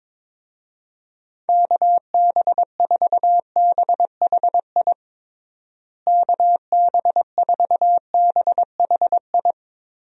Morse code, heard as a keyed tone, about 22 wpm, 700 Hz. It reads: KB4BHI KB4BHI
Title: USA Call Signs - 6 Characters - Repeated 2x - 22wpm